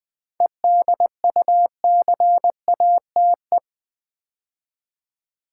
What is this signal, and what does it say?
Morse code, heard as a keyed tone, about 20 wpm, 700 Hz. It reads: EDUCATE
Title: 7 Letter Words - First 3 Letters Given - 20wpm